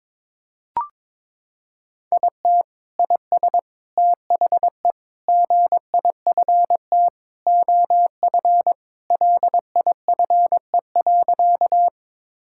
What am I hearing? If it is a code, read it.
Morse code, heard as a keyed tone, about 21 wpm, 700 Hz. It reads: IT IS THE GIFT OF LIFE.
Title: Sentences from Top 1000 Words - Review Part 6 - 22wpm